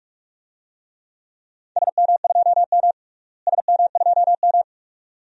Morse code, heard as a keyed tone, about 45 wpm, 700 Hz. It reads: SM2M SM2M
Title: DXCC-100 Call Signs - 4 Characters - Repeated 2x - 45wpm